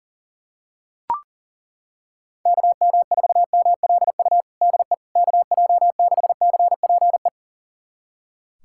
Morse code, heard as a keyed tone, about 40 wpm, 700 Hz. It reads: KM4MLU DE KJ6CPE
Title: USA Callsign DE Callsign 40wpm